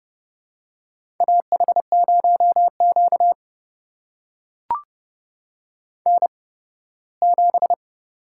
Morse code, heard as a keyed tone, about 31 wpm, spaced as for 18 wpm, 700 Hz. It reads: AH0Q N7
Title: USA Call Signs - 4 Characters - 8x Character Spacing - 30wpm